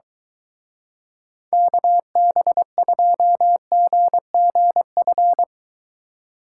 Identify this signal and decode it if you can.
Morse code, heard as a keyed tone, about 23 wpm, 700 Hz. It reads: KB2GGF